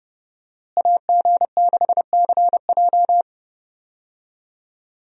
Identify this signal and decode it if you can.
Morse code, heard as a keyed tone, about 30 wpm, 700 Hz. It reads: AG6CJ
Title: USA Call Signs - 5 Characters - Repeated 3x - 30wpm